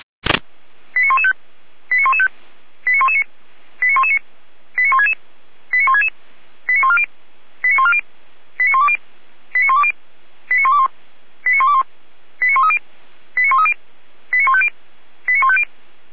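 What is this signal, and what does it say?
Five-tone selective call (ZVEI), heard as quick runs of five tones, tones of about 70 ms. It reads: 89115 89115 89119 89119 89166 89166 89140 89140 89120 89120 89121 89121 89130 89130 89150 89150
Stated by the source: ZVEI five-tone call received by SDR, logged on the feuerwehr-orlamuende.org server.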